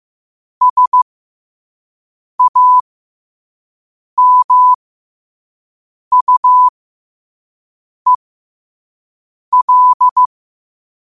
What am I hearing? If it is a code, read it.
Morse code, heard as a keyed tone, about 15 wpm, spaced as for 5 wpm, 1000 Hz. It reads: SAMUEL
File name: qotdfiles / 5wpm